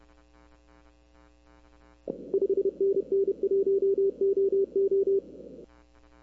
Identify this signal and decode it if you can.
Morse code, heard as a keyed tone, about 30 wpm, 390 Hz. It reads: 5NN 1OO
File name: arrl cw dx